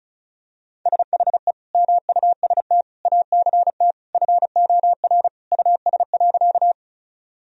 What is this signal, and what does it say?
Morse code, heard as a keyed tone, about 34 wpm, 700 Hz. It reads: SHE MUST ACT FOR US.
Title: Sentences from Top 1000 Words - Review Part 1 - 35wpm